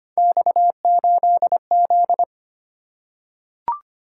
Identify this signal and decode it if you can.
Morse code, heard as a keyed tone, about 25 wpm, 700 Hz. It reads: X8Z
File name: Three Letter-Number Combinations - 25wpm